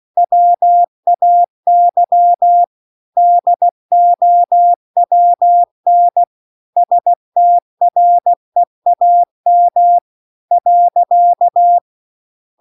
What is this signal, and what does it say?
Morse code, heard as a keyed tone, about 16 wpm, 700 Hz. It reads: WAY DOWN STREAM .